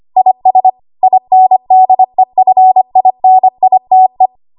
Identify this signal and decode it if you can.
Morse code, heard as a keyed tone, about 25 wpm, 750 Hz. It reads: IS INDEFINITE
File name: Archive / 25 WPM / 25WPM